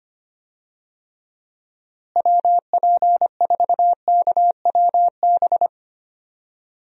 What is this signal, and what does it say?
Morse code, heard as a keyed tone, about 25 wpm, 700 Hz. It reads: WP4KWB
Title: USA Call Signs - 6 Characters - Repeated 3x - 25wpm